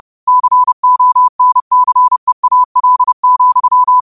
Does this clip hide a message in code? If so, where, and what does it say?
Yes: Morse code from 0.3 to 4.0 s, heard as a keyed tone, about 30 wpm, 1000 Hz. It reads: CONCEAL,